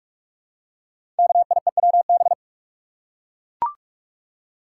Morse code, heard as a keyed tone, about 44 wpm, 690 Hz. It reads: KIEWB
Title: Code Groups - 5 Letters - 45wpm